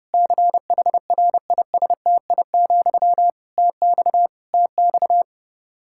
Morse code, heard as a keyed tone, about 32 wpm, 700 Hz. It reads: CHRISTI, TX TX
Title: K1USN SST QSO Practice 30wpm